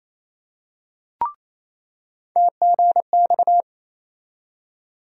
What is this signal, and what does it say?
Morse code, heard as a keyed tone, about 28 wpm, 700 Hz. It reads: TGX
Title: USA Call Sign Postfixes - 3 Letters 28wpm